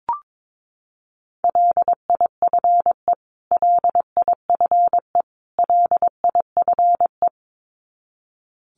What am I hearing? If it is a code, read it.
Morse code, heard as a keyed tone, about 22 wpm, 700 Hz. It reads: LIFE LIFE LIFE
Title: Top 500 Words Repeated 22wpm